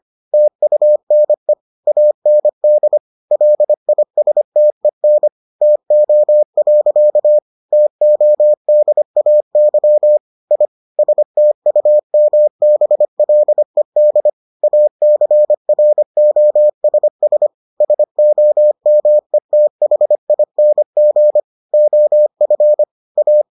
Morse code, heard as a keyed tone, about 25 wpm, 600 Hz. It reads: TUNE AND LISTEN TO. TODAY I STUMBLED ACROSS SOMETHING OF A